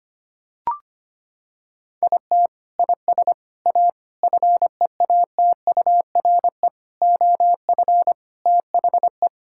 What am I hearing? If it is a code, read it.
Morse code, heard as a keyed tone, about 25 wpm, 700 Hz. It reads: IT IS A FEATURE OF THE
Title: Sentences from Top 1600 Words 25wpm